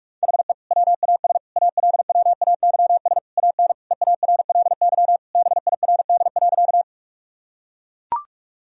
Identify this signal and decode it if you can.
Morse code, heard as a keyed tone, about 44 wpm, 700 Hz. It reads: HE WAS ALWAYS AN EARLY BIRD.
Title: Sentences from Top 1000 Words - Review Part 2 - 45wpm